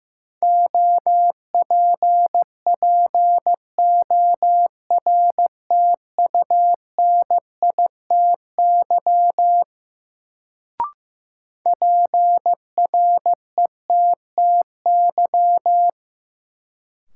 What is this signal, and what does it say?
Morse code, heard as a keyed tone, about 16 wpm, 700 Hz. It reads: OPPORTUNITY PRETTY
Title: Sentences from Top 1000 Words - New Word Drill - 15wpm